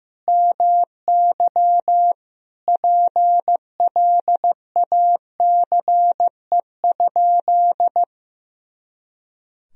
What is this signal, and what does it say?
Morse code, heard as a keyed tone, about 15 wpm, 700 Hz. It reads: MY PLACE?
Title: Sentences from Top 200 Words and Abbreviations 15wpm